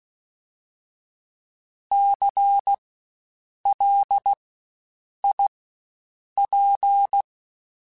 Morse code, heard as a keyed tone, about 16 wpm, spaced as for 7 wpm, 800 Hz. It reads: CLIP
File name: Spaced Alphas / Spaced Alphas mp3s / Cs sp @15